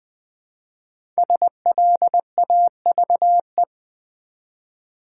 Morse code, heard as a keyed tone, about 20 wpm, 700 Hz. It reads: SLAVE
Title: Build A Word - 1 Letter At A Time - 3 to 5-Letter Words - 20wpm